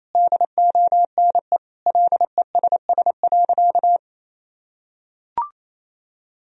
Morse code, heard as a keyed tone, about 28 wpm, 700 Hz. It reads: DONE LESS.
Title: Sentences from Top 1000 Words - Review Part 3 - 28wpm